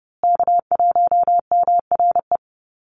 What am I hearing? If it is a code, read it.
Morse code, heard as a keyed tone, about 30 wpm, 700 Hz. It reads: K1MRE
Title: USA Call Signs Repeated 2x 30wpm